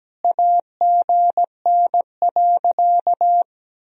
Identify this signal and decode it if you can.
Morse code, heard as a keyed tone, about 17 wpm, 700 Hz. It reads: AGN.
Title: Sentences from Top 300 Words and Abbreviations 17wpm